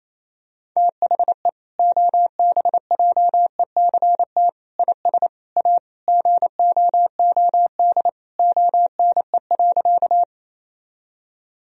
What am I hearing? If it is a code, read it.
Morse code, heard as a keyed tone, about 28 wpm, 700 Hz. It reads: THE OBJECT IS A GOOD ONE.
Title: Sentences from Top 1000 Words - Review Part 3 - 28wpm